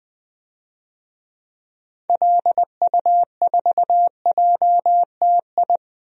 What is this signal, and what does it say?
Morse code, heard as a keyed tone, about 20 wpm, 700 Hz. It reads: LU4JTI
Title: DXCC-100 Call Signs - 6 Characters - 20wpm